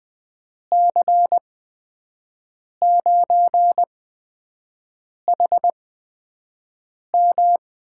Morse code, heard as a keyed tone, about 20 wpm, spaced as for 5 wpm, 700 Hz. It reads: C9HM
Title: USA Call Signs - 5 Characters - 8x Character Spacing - 20wpm